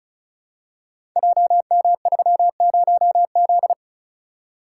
Morse code, heard as a keyed tone, about 36 wpm, 700 Hz. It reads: JM30Z
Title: Code Groups - 5 Letter:Numbers - 35wpm